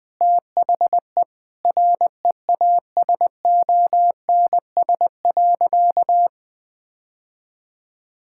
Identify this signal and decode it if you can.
Morse code, heard as a keyed tone, about 21 wpm, 700 Hz. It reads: THE REASONS.
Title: Sentences with 4-Word Phrases - 20wpm